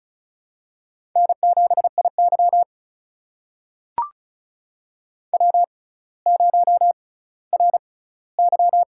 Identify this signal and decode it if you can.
Morse code, heard as a keyed tone, about 36 wpm, spaced as for 18 wpm, 700 Hz. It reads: N7IY W0RY
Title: USA Call Signs - 4 Characters - 6x Character Spacing - 35wpm